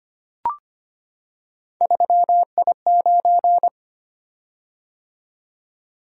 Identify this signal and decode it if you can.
Morse code, heard as a keyed tone, about 25 wpm, 700 Hz. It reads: 3I9
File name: Three Letter-Number Combinations - 25wpm